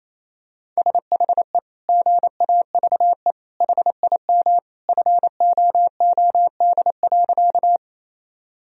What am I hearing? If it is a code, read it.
Morse code, heard as a keyed tone, about 28 wpm, 700 Hz. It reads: SHE GAVE HIM FOOD.
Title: Sentences from Top 1000 Words - Review Part 1 - 28wpm